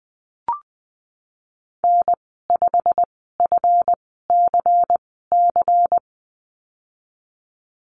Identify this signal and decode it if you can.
Morse code, heard as a keyed tone, about 20 wpm, spaced as for 15 wpm, 700 Hz. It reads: N5FCC
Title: 186 Lesson 30g - Letter b - Call Signs 20wpm